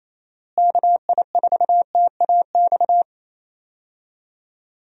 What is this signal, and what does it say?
Morse code, heard as a keyed tone, about 28 wpm, 700 Hz. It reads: KI4TAX
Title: USA Call Signs - 6 Characters - Repeated 3x - 28wpm